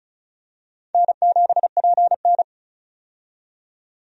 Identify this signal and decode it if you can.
Morse code, heard as a keyed tone, about 36 wpm, 700 Hz. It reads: N7PN
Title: USA Call Signs - 4 Characters - 35wpm